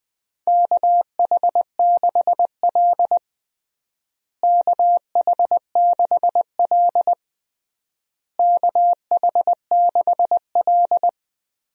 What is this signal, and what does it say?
Morse code, heard as a keyed tone, about 20 wpm, 700 Hz. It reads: KH6L KH6L KH6L
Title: DXCC-100 Call Signs - 4 Characters - Repeated 3x - 20wpm